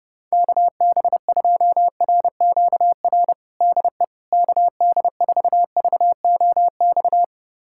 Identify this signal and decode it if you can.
Morse code, heard as a keyed tone, about 30 wpm, 700 Hz. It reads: KB2RQR DE KD4VOX
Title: USA Callsign DE Callsign - 6 Characters - 30wpm